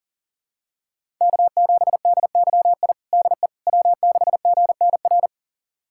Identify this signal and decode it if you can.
Morse code, heard as a keyed tone, about 40 wpm, 700 Hz. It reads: K7DYI DE W6GNR
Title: USA Callsign DE Callsign - 5 Characters - 40wpm